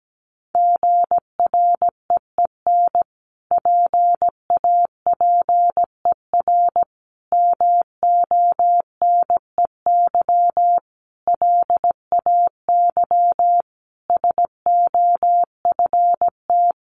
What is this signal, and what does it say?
Morse code, heard as a keyed tone, about 17 wpm, 700 Hz. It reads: GREEN PAPER MONEY LAY SOFT
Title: Sentences from Top 1500 Words 17wpm